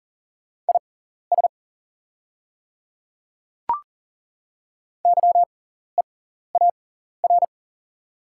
Morse code, heard as a keyed tone, about 40 wpm, spaced as for 14 wpm, 690 Hz. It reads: IS YEAR
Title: 6x Character Spacing - Top 300 Words - 40wpm